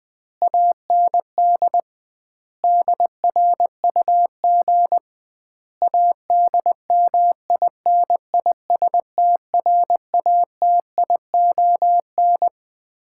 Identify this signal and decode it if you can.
Morse code, heard as a keyed tone, about 21 wpm, 700 Hz. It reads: AND DRUG ADMINISTRATION
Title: 2x Word-Spacing - Sets of 5 Words - 20wpm